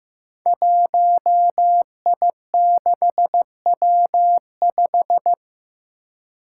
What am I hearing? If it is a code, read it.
Morse code, heard as a keyed tone, about 15 wpm, 700 Hz. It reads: 1I6W5
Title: Five Challenging Letter-Number Combinations 15wpm